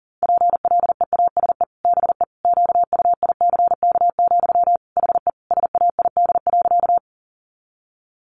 Morse code, heard as a keyed tone, about 40 wpm, 700 Hz. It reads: PLEASE BE QUICK, HE SAID.
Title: Sentences from Top 1500 Words 40wpm